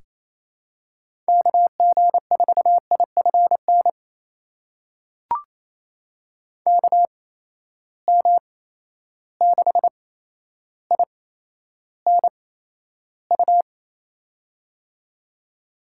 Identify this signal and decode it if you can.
Morse code, heard as a keyed tone, about 28 wpm, spaced as for 12 wpm, 700 Hz. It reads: KG4IFN KM6INU
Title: USA Call Signs - 6 Characters - 8x Character Spacing - 28wpm